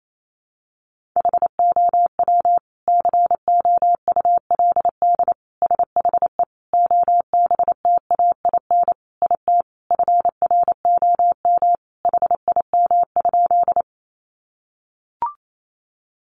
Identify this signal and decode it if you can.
Morse code, heard as a keyed tone, about 28 wpm, 700 Hz. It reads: HOW COULD SHE OBTAIN IT FROM HIM?
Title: Sentences from Top 1300 Words 28wpm